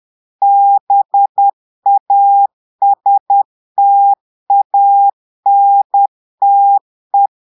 Morse code, heard as a keyed tone, about 10 wpm, 800 Hz. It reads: BASTANTE
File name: Portugais - texte en clair 1 - vitesse - 10 mots minute